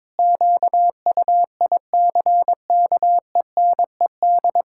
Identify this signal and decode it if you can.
Morse code, heard as a keyed tone, about 22 wpm, 700 Hz. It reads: QUICKENED
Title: Pangram Challenge v2 - 22wpm